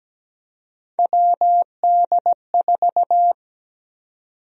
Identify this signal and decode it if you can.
Morse code, heard as a keyed tone, about 17 wpm, 700 Hz. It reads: WD4
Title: USA Call Sign Prefixes - 17wpm